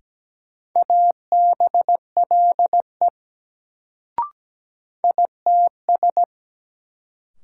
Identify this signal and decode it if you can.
Morse code, heard as a keyed tone, about 17 wpm, 700 Hz. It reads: ABLE ITS
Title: Sets of 2 Words - New Word Drill - 17wpm